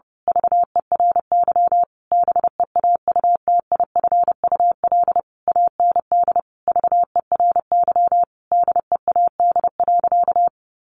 Morse code, heard as a keyed tone, about 28 wpm, 700 Hz. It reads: VERY BEAUTIFUL AND VERY DEAD.
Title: A Sound Of Thunder 30wpm